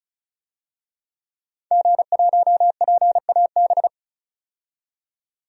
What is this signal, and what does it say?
Morse code, heard as a keyed tone, about 36 wpm, 700 Hz. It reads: G1PAB
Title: DXCC-100 Call Signs - 5 Characters - 35wpm